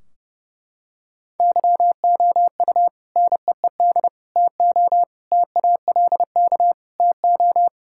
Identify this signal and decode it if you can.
Morse code, heard as a keyed tone, about 30 wpm, 700 Hz. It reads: YOU NEED TO TALK TO
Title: Sentences from Top 1600 Words 30wpm